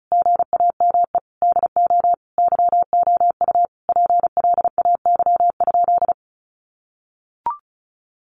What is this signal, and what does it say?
Morse code, heard as a keyed tone, about 34 wpm, 700 Hz. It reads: GAME DO YOU PLAY?
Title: Sentences from Top 1400 Words 35wpm